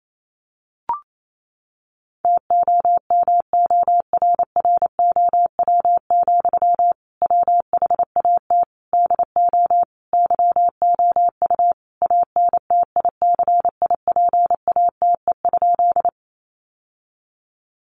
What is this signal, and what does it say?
Morse code, heard as a keyed tone, about 28 wpm, 700 Hz. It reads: TOMORROW, WHAT DO YOU ANTICIPATE?
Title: Sentences from Top 1500 Words 28wpm